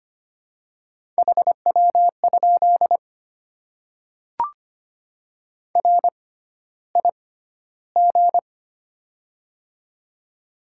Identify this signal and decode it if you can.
Morse code, heard as a keyed tone, about 25 wpm, spaced as for 13 wpm, 690 Hz. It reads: HW? RIG
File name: Top 20 QSO Elements - 6x Character Spacing - 25wpm